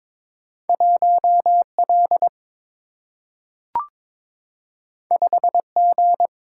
Morse code, heard as a keyed tone, about 22 wpm, 700 Hz. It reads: 1 L 5 G
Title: Single-Letter Single-Number 22wpm